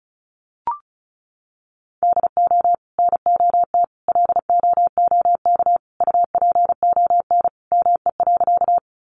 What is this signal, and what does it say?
Morse code, heard as a keyed tone, about 34 wpm, 700 Hz. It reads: DO NOT LOOK UPON ME.
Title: Sentences from Top 1100 Words 35wpm